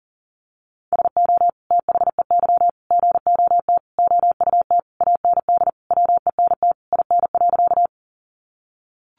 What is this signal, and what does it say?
Morse code, heard as a keyed tone, about 40 wpm, 700 Hz. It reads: SO THEY GOT OUT AND WENT IN.
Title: Sentences from Top 300 Words 40wpm